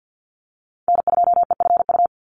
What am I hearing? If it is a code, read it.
Morse code, heard as a keyed tone, about 50 wpm, 700 Hz. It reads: N2EFU
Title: USA Call Signs 50wpm